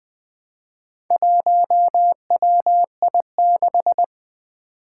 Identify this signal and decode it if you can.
Morse code, heard as a keyed tone, about 20 wpm, 700 Hz. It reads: 1WI6
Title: Four Challenging Letter-Number Combinations 20wpm